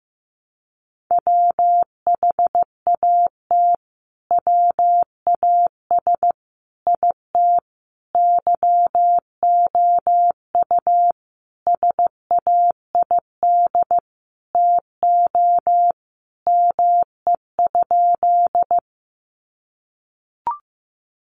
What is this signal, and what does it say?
Morse code, heard as a keyed tone, about 15 wpm, 700 Hz. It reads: WHAT WAS IT YOU SAID TO ME?
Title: Sentences from Top 200 Words 15wpm